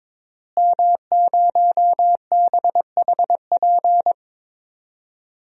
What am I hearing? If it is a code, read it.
Morse code, heard as a keyed tone, about 22 wpm, 700 Hz. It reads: M0BHP
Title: DXCC-100 Call Signs - 5 Characters - Repeated 3x - 22wpm